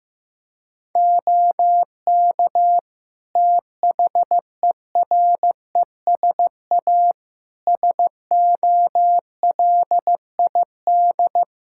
Morse code, heard as a keyed tone, about 15 wpm, 700 Hz. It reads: OK THERESA SOLID
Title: Practice QSO - Exchange 3 - More Info - 15wpm